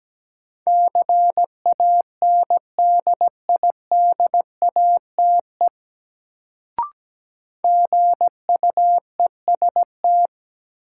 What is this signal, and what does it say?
Morse code, heard as a keyed tone, about 17 wpm, 700 Hz. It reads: CANDIDATE GUEST
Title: Sentences from Top 1600 Words - New Word Drill Alt - 17wpm